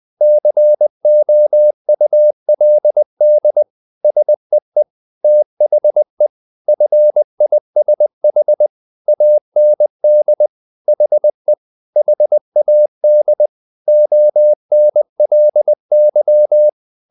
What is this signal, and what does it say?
Morse code, heard as a keyed tone, about 20 wpm, 600 Hz. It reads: COULD SEE THE FISH AND HE HAD ONLY
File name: the old man and the sea part 2 20wpm